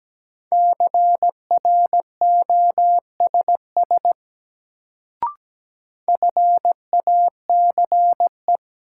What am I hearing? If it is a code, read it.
Morse code, heard as a keyed tone, about 17 wpm, 700 Hz. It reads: CROSS FACE
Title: Sentences from Top 300 Words - New Word Drill - 17wpm